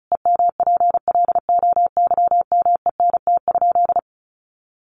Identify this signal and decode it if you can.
Morse code, heard as a keyed tone, about 34 wpm, 700 Hz. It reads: EMPLOYMENT?
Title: Sentences from Top 1500 Words 35wpm